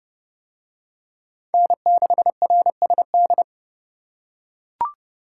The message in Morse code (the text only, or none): N6RSD